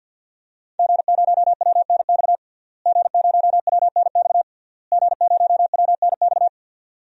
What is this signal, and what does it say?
Morse code, heard as a keyed tone, about 51 wpm, 700 Hz. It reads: G0WNX G0WNX G0WNX
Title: DXCC-100 Call Signs - 5 Characters - Repeated 3x - 50wpm